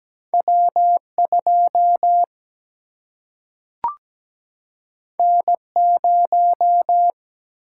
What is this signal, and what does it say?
Morse code, heard as a keyed tone, about 17 wpm, 700 Hz. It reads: W2 N0
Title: Single-Letter Single-Number 17wpm